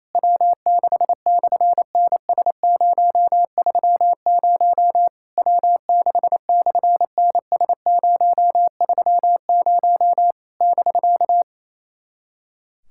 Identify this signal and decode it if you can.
Morse code, heard as a keyed tone, about 28 wpm, 700 Hz. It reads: W6/NS030 W6/NS030 <BK>
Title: SOTA QSO Practice 28wpm